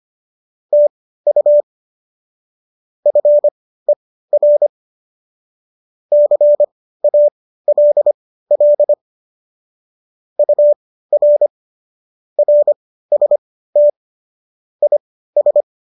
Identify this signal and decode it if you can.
Morse code, heard as a keyed tone, about 25 wpm, spaced as for 15 wpm, 590 Hz. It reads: TU FER CALL UR RST IS